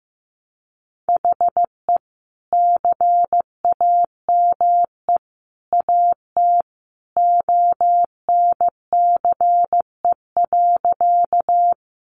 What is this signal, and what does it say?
Morse code, heard as a keyed tone, about 15 wpm, 700 Hz. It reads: HE CAME AT ONCE.
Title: Sentences from Top 300 Words 15wpm